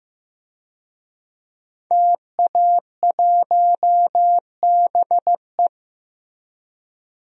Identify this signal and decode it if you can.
Morse code, heard as a keyed tone, about 15 wpm, 700 Hz. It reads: TA1BE
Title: DXCC-100 Call Signs - 5 Characters - Repeated 3x - 15wpm